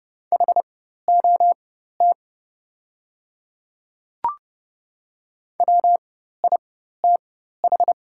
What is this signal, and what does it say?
Morse code, heard as a keyed tone, about 29 wpm, spaced as for 14 wpm, 700 Hz. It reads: HOT WITH
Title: 4x Character Spacing - Top 500 Words - 30wpm